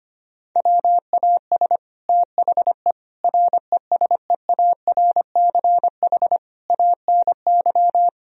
Morse code, heard as a keyed tone, about 25 wpm, 700 Hz. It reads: WAS THE RESEARCH ANY
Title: Sentences from Top 1000 Words - Review Part 4 - 25wpm